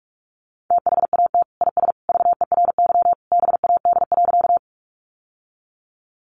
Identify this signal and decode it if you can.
Morse code, heard as a keyed tone, about 44 wpm, 700 Hz. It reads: THAT IS VERY BAD.
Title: Sentences from Top 700 Words 45wpm